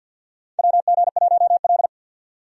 Morse code, heard as a keyed tone, about 52 wpm, 700 Hz. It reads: WG1L
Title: USA Call Signs - 4 Characters - 50wpm